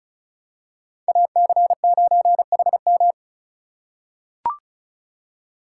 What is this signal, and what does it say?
Morse code, heard as a keyed tone, about 36 wpm, 700 Hz. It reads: AC9HM
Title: USA Call Signs - 5 Characters - 2x Character Spacing - 35wpm